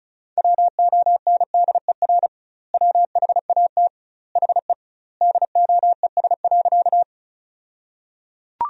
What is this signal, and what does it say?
Morse code, heard as a keyed tone, about 34 wpm, 700 Hz. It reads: WONDER WHAT HE DOES.
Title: 2x Word-Spacing - Sentences from Top 500 Words - 35wpm